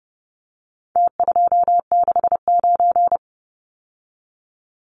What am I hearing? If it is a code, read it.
Morse code, heard as a keyed tone, about 30 wpm, 700 Hz. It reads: T269
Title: Numbers - Contest Style - 4-Digit 30wpm